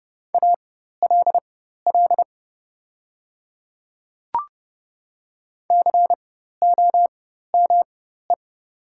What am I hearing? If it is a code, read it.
Morse code, heard as a keyed tone, about 31 wpm, spaced as for 14 wpm, 700 Hz. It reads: ALL COME
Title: 4x Character Spacing - Top 100 Words - 30wpm